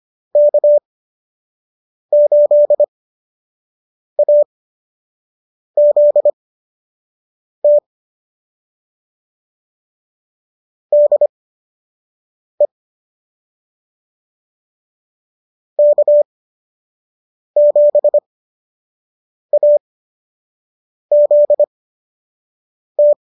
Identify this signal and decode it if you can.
Morse code, heard as a keyed tone, about 25 wpm, spaced as for 6 wpm, 590 Hz. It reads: K8AZT DE K7AZT